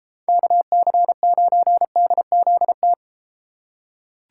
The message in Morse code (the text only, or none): KC9DZT